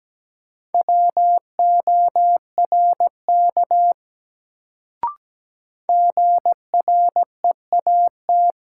Morse code, heard as a keyed tone, about 17 wpm, 700 Hz. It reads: WORK GREAT
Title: Sentences from Top 200 Words - New Word Drill - 17wpm